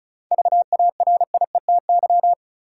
Morse code, heard as a keyed tone, about 36 wpm, 700 Hz. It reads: VARIETY
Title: Sets of 3 Words - New Word Drill - 35wpm